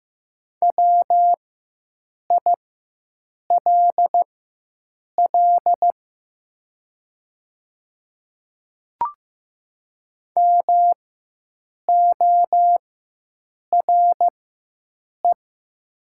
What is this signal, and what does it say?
Morse code, heard as a keyed tone, about 15 wpm, spaced as for 7 wpm, 700 Hz. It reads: WILL MORE
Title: 4x Character Spacing - Top 200 Words - 15wpm